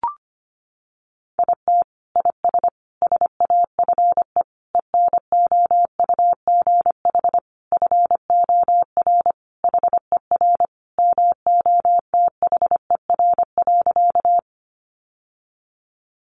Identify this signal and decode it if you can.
Morse code, heard as a keyed tone, about 25 wpm, 700 Hz. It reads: IT IS SAFE ENOUGH FOR HER MOTHER.
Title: Sentences from Top 900 Words 25wpm